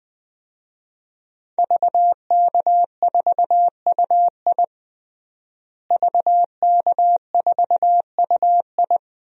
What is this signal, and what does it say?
Morse code, heard as a keyed tone, about 20 wpm, 700 Hz. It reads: VK4UI VK4UI
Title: DXCC-100 Call Signs - 5 Characters - Repeated 2x - 20wpm